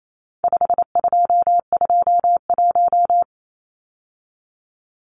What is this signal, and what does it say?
Morse code, heard as a keyed tone, about 28 wpm, 700 Hz. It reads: 5221
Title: Numbers - 4-Digit 28wpm